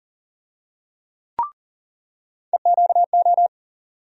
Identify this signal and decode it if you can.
Morse code, heard as a keyed tone, about 40 wpm, 700 Hz. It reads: EQO
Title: USA Call Sign Postfixes - 3 Letters 40wpm